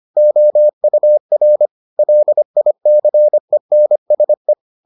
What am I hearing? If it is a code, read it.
Morse code, heard as a keyed tone, about 25 wpm, 600 Hz. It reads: OUR LICENSE